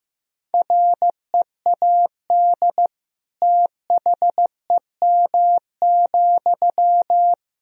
Morse code, heard as a keyed tone, about 15 wpm, 700 Hz. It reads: READ THEM,
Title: Sentences from Top 400 Words and Abbreviations 15wpm